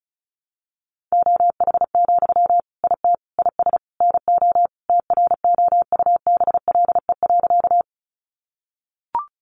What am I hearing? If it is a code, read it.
Morse code, heard as a keyed tone, about 34 wpm, 700 Hz. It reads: OH, IT IS NO TROUBLE.
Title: Sentences from Top 1200 Words 35wpm